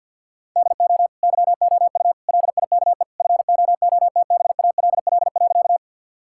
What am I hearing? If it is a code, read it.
Morse code, heard as a keyed tone, about 50 wpm, 700 Hz. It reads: DO YOU LIKE FOOTBALL.
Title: Sentences from Top 1600 Words 50wpm